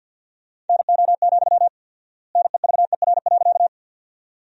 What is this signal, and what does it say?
Morse code, heard as a keyed tone, about 50 wpm, 700 Hz. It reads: NO, NEVER.
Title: 4x Word-Spacing - Sentences from Top 300 Words - 50wpm